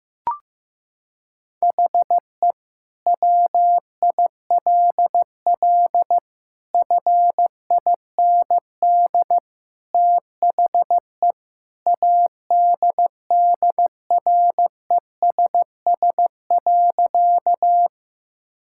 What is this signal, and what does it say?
Morse code, heard as a keyed tone, about 15 wpm, 700 Hz. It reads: HE WILL FIND THE ADDRESS.
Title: Sentences from Top 1000 Words - Review Part 6 - 15wpm